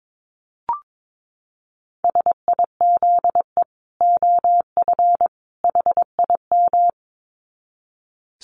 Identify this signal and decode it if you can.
Morse code, heard as a keyed tone, about 22 wpm, 700 Hz. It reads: SIZE OF HIM
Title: Dit Challenge 22wpm